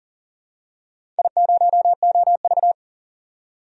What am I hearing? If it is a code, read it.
Morse code, heard as a keyed tone, about 40 wpm, 700 Hz. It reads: I0OV
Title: DXCC-100 Call Signs - 4 Characters - 40wpm